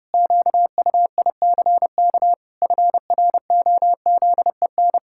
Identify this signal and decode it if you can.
Morse code, heard as a keyed tone, about 32 wpm, 700 Hz. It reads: QUICK FROZEN
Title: Pangram Challenge v2 - 30wpm